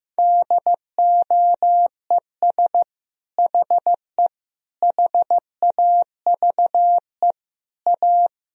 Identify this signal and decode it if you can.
Morse code, heard as a keyed tone, about 15 wpm, 700 Hz. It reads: DOES HE HAVE A
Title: Sentences from Top 1000 Words - Review Part 3 - 15wpm